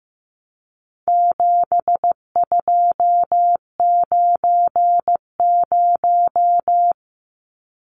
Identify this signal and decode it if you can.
Morse code, heard as a keyed tone, about 15 wpm, 700 Hz. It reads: 7290
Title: Numbers - 4-Digit 15wpm